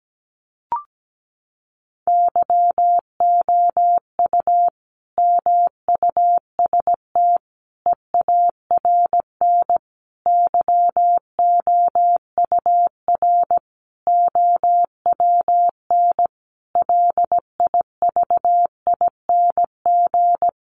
Morse code, heard as a keyed tone, about 17 wpm, 700 Hz. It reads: YOU MUST EARN YOUR OWN LIVING
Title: Sentences from Top 1500 Words 17wpm